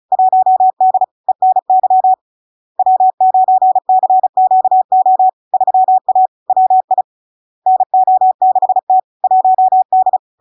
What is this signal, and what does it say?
Morse code, heard as a keyed tone, about 36 wpm, 750 Hz. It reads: 1D ENY W9CQO 3A WI NO6T 1D
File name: LARK-FD-2022-Log-in-CW-35wpm-750-Hz